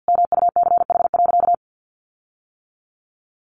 Morse code, heard as a keyed tone, about 50 wpm, 700 Hz. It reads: MUCH.
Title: Sentences from Top 1500 Words 50wpm